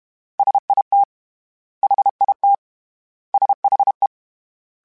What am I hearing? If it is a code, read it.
Morse code, heard as a keyed tone, about 32 wpm, 800 Hz. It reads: SIT HIT SHE
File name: S vs H / S vs H mp3s / S vs H @30